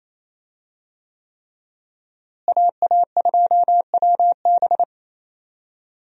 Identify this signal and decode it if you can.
Morse code, heard as a keyed tone, about 28 wpm, 700 Hz. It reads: AA2WB
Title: USA Call Signs - 5 Characters - Repeated 3x - 28wpm